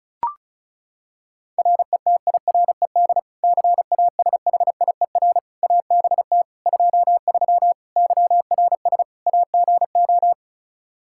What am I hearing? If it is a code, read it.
Morse code, heard as a keyed tone, about 34 wpm, 700 Hz. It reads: RETIRED CASHIER ABT 23 YRS AGO
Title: Practice QSO - Exchange 3 - More Info - 35wpm